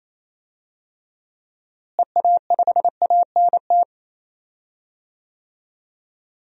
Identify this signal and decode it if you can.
Morse code, heard as a keyed tone, about 28 wpm, 700 Hz. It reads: EA5ANT
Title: DXCC-100 Call Signs - 6 Characters - Repeated 2x - 28wpm